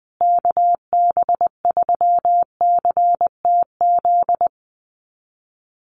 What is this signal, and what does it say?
Morse code, heard as a keyed tone, about 20 wpm, 700 Hz. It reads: KB3CTZ
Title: USA Call Signs Repeated 3x 20wpm